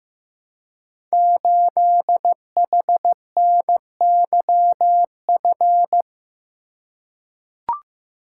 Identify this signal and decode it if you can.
Morse code, heard as a keyed tone, about 15 wpm, 700 Hz. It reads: 8HNYF
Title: Code Groups - 5 Letter:Numbers - 15wpm